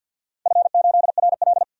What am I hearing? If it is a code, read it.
Morse code, heard as a keyed tone, about 48 wpm, 700 Hz. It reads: F8RL
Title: DXCC-100 Call Signs - 4 Characters - 50wpm